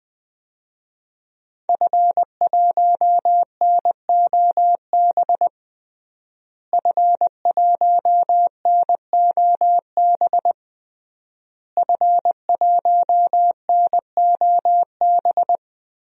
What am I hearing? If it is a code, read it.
Morse code, heard as a keyed tone, about 20 wpm, 700 Hz. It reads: F1NOB F1NOB F1NOB
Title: DXCC-100 Call Signs - 5 Characters - Repeated 3x - 20wpm